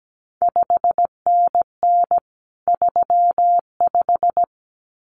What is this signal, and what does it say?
Morse code, heard as a keyed tone, about 17 wpm, 700 Hz. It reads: 5NN 35
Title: CQ WW DX Contest Exchange Encore - 17wpm